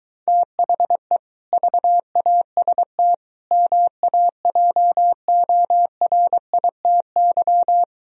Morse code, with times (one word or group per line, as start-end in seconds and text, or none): 0.3-1.2 THE
1.5-3.1 VAST
3.5-7.8 MAJORITY